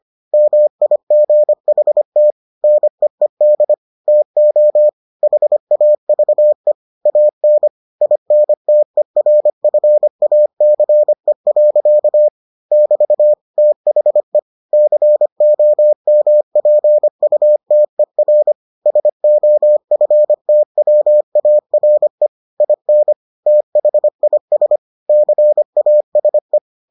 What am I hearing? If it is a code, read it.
Morse code, heard as a keyed tone, about 25 wpm, 600 Hz. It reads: MIGHT NEED TO HAVE AN INTERFACE. <BT> THE COMPUTER SOFTWARE IN THIS CASE